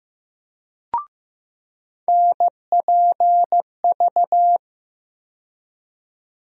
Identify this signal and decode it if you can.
Morse code, heard as a keyed tone, about 15 wpm, 700 Hz. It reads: NPV
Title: USA Call Sign Postfixes - 3 Letters 15wpm